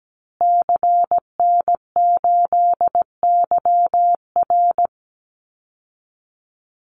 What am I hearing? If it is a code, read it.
Morse code, heard as a keyed tone, about 17 wpm, 700 Hz. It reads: CN8YR
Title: Call Signs Repeated 2x - 17wpm